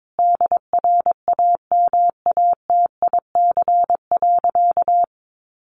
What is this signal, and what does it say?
Morse code, heard as a keyed tone, about 22 wpm, 700 Hz. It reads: DRAMATIC.
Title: Sentences from Top 1300 Words 22wpm